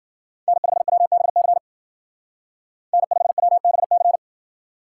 Morse code, heard as a keyed tone, about 52 wpm, 700 Hz. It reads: N5WBC N5WBC